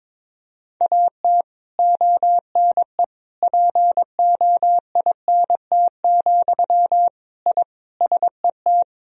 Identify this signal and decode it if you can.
Morse code, heard as a keyed tone, about 22 wpm, 700 Hz. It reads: AT ONE POINT, I SET
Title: Sentences from Top 1000 Words - Review Part 1 - 22wpm